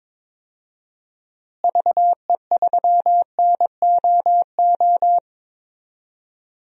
Morse code, heard as a keyed tone, about 22 wpm, 700 Hz. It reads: VE3NOO
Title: DXCC-100 Call Signs - 6 Characters - Repeated 2x - 22wpm